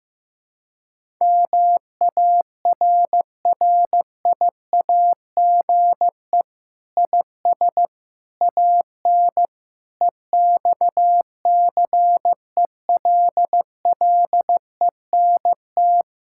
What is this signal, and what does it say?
Morse code, heard as a keyed tone, about 15 wpm, 700 Hz. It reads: MARRIAGE IS AN EXCELLENT
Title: Sentences from Top 1000 Words - Review Part 5 - 15wpm